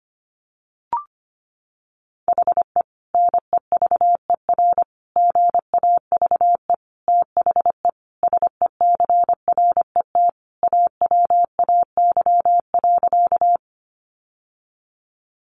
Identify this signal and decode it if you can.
Morse code, heard as a keyed tone, about 25 wpm, 700 Hz. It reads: HE NEVER GAVE THE SECRET AWAY.
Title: Sentences from Top 1200 Words 25wpm